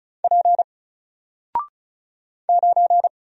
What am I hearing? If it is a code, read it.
Morse code, heard as a keyed tone, about 34 wpm, 700 Hz. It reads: P 9
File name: Single-Letter-Number-35wpm